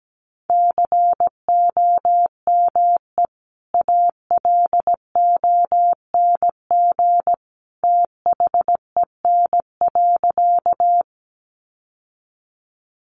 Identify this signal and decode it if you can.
Morse code, heard as a keyed tone, about 17 wpm, 700 Hz. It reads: COME ALONG THEN.
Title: Sentences from Top 700 Words 17wpm